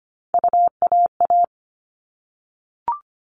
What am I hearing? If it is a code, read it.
Morse code, heard as a keyed tone, about 25 wpm, 700 Hz. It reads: UAA
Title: All Cut Numbers - 3-Digit 25wpm